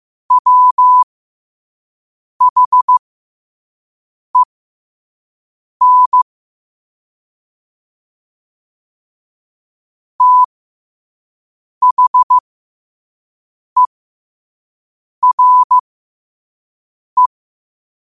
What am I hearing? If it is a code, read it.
Morse code, heard as a keyed tone, about 15 wpm, spaced as for 5 wpm, 1000 Hz. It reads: WHEN THERE
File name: qotdfiles / 5wpm